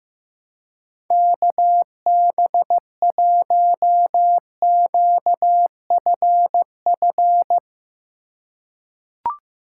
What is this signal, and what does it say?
Morse code, heard as a keyed tone, about 15 wpm, 700 Hz. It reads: KB1QFF
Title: USA Call Signs - 6 Characters - 2x Character Spacing - 15wpm